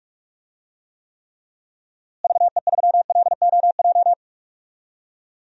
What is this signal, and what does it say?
Morse code, heard as a keyed tone, about 41 wpm, 700 Hz. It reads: VE3LOJ
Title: DXCC-100 Call Signs - 6 Characters - Repeated 3x - 45wpm